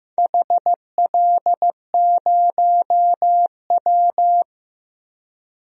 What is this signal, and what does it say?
Morse code, heard as a keyed tone, about 15 wpm, 700 Hz. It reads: HL0W
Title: DXCC-100 Call Signs - 4 Characters - 15wpm